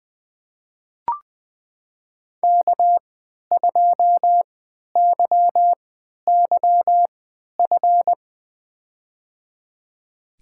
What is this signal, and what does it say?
Morse code, heard as a keyed tone, about 21 wpm, spaced as for 11 wpm, 700 Hz. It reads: K2YYF
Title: USA Call Signs - 5 Characters - 3x Character Spacing - 20wpm